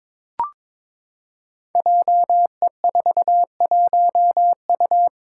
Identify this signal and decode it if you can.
Morse code, heard as a keyed tone, about 22 wpm, 700 Hz. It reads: JE41U
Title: Five Challenging Letter-Number Combinations 22wpm